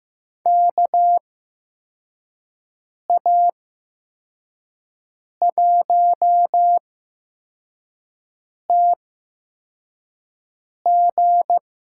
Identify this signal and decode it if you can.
Morse code, heard as a keyed tone, about 15 wpm, spaced as for 4 wpm, 700 Hz. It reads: KA1TG
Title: USA Call Signs - 8x Character Spacing - 15wpm